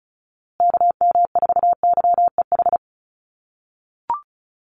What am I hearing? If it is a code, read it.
Morse code, heard as a keyed tone, about 34 wpm, 700 Hz. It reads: KM4YEH
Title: USA Call Signs Repeated 2x 35wpm